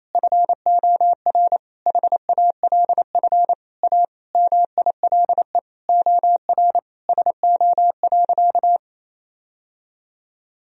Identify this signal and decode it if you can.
Morse code, heard as a keyed tone, about 27 wpm, 700 Hz. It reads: FOR HALF A MILE OR SO.
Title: Sentences from Top 1000 Words - Review Part 1 - 28wpm